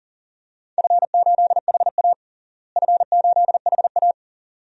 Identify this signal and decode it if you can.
Morse code, heard as a keyed tone, about 37 wpm, 700 Hz. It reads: F8HA F8HA
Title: DXCC-100 Call Signs - 4 Characters - Repeated 2x - 40wpm